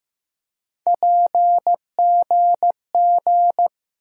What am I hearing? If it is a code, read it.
Morse code, heard as a keyed tone, about 15 wpm, 700 Hz. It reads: PGG
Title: USA Call Sign Postfixes - 3 Letters 15wpm